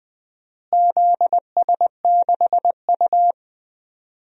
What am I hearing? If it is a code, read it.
Morse code, heard as a keyed tone, about 20 wpm, 700 Hz. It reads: ZS6U
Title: DXCC-100 Call Signs - 4 Characters - Repeated 2x - 20wpm